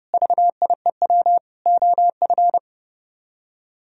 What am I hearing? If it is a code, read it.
Morse code, heard as a keyed tone, about 37 wpm, 700 Hz. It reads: VIEW OF
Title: Speed-Racing - Sets of 5 Words 30wpm